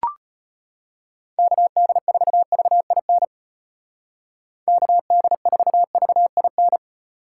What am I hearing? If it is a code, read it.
Morse code, heard as a keyed tone, about 31 wpm, 700 Hz. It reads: KD4VIN KD4VIN
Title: Speed Racing - USA 2x3 Call Signs 25wpm